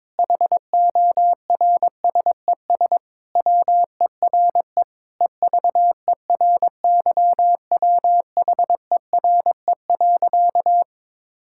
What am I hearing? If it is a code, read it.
Morse code, heard as a keyed tone, about 22 wpm, 700 Hz. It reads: HORSES WERE EVERYWHERE.
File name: practice / 22wpm / Sentences with 8-Word Phrases - 22wpm